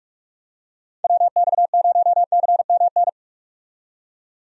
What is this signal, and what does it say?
Morse code, heard as a keyed tone, about 46 wpm, 700 Hz. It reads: WX0CMN
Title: USA Call Signs - 6 Characters - 45wpm